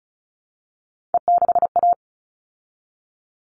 Morse code, heard as a keyed tone, about 34 wpm, 700 Hz. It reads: E6A
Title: All Cut Numbers - 3-Digit 35wpm